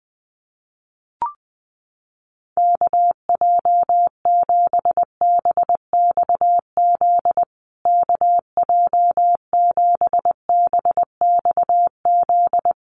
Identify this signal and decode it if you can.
Morse code, heard as a keyed tone, about 20 wpm, 700 Hz. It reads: KJ7BXZ KJ7BXZ
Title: USA Call Signs Repeated 3x 20wpm